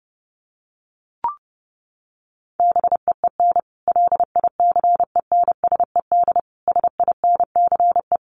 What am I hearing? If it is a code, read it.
Morse code, heard as a keyed tone, about 28 wpm, 700 Hz. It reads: BEEN LICENSED SINCE
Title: Common QSO Phrases 30wpm